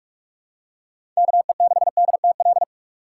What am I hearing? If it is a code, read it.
Morse code, heard as a keyed tone, about 47 wpm, 700 Hz. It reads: KE6DTL
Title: USA Call Signs - 6 Characters - 2x Character Spacing - 45wpm